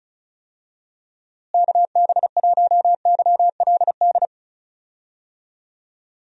Morse code, heard as a keyed tone, about 34 wpm, 700 Hz. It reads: KB1YLD
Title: USA Call Signs - 6 Characters - Repeated 2x - 35wpm